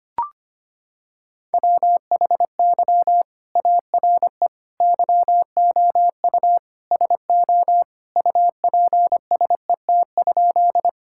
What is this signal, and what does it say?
Morse code, heard as a keyed tone, about 25 wpm, 700 Hz. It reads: WHY ARE YOU SO UPSET?
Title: Sentences from Top 1000 Words - Review Part 5 - 25wpm